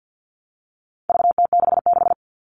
Morse code, heard as a keyed tone, about 50 wpm, 700 Hz. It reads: VT66
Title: All Cut Numbers - 4-Digit 50wpm